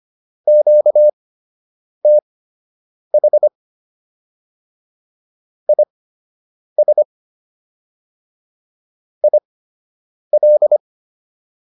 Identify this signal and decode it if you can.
Morse code, heard as a keyed tone, about 25 wpm, spaced as for 8 wpm, 600 Hz. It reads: QTH IS IL